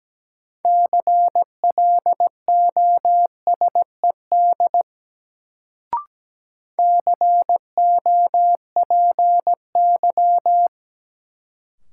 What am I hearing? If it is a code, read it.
Morse code, heard as a keyed tone, about 17 wpm, 700 Hz. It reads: CLOSED COPY
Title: Sentences from Top 1400 Words - New Word Drill Alt - 17wpm